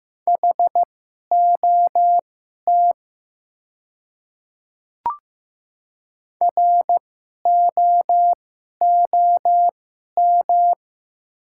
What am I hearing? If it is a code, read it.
Morse code, heard as a keyed tone, about 15 wpm, spaced as for 11 wpm, 700 Hz. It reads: HOT ROOM
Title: 2x Character Spacing - Top 300 Words - 15wpm